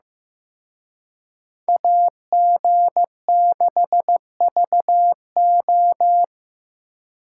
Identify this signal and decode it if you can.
Morse code, heard as a keyed tone, about 18 wpm, 700 Hz. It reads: AG6VO
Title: Speed Racing - USA 2x2 Call Signs 15wpm